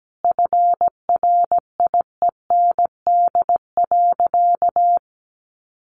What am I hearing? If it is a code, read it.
Morse code, heard as a keyed tone, about 17 wpm, 700 Hz. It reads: FRIEND.
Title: Sentences from Top 1400 Words 17wpm